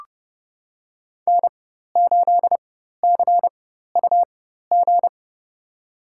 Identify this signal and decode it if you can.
Morse code, heard as a keyed tone, about 31 wpm, spaced as for 14 wpm, 700 Hz. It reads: N8CUG
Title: USA Call Signs - 5 Characters - 4x Character Spacing - 30wpm